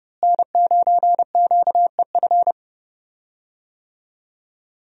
Morse code, heard as a keyed tone, about 30 wpm, 700 Hz. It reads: N9QEF
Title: USA Call Signs - 5 Characters - Repeated 2x - 30wpm